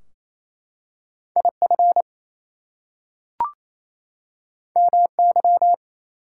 Morse code, heard as a keyed tone, about 29 wpm, 700 Hz. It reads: IF MY
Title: Two Letter Words - 28wpm